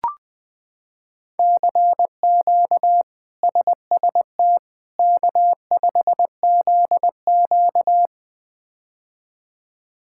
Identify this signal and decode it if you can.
Morse code, heard as a keyed tone, about 21 wpm, 700 Hz. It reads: CQ SST K5ZQ
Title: K1USN SST QSO Practice - 20wpm